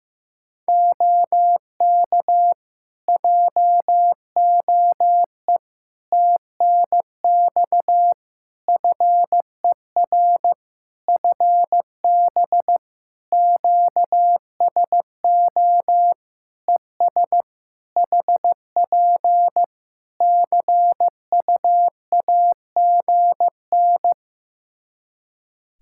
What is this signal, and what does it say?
Morse code, heard as a keyed tone, about 15 wpm, 700 Hz. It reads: OK JOE TNX FER FB QSO ES HP CUAGN
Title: Practice QSO - Exchange 4 - Ending - 15wpm